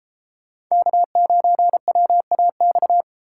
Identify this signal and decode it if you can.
Morse code, heard as a keyed tone, about 36 wpm, 700 Hz. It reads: K9WAX